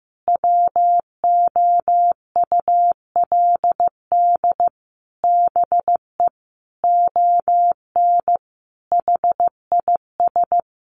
Morse code, heard as a keyed tone, about 15 wpm, 700 Hz. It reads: WOULD BE ON HIS
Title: Sentences from Top 300 Words 15wpm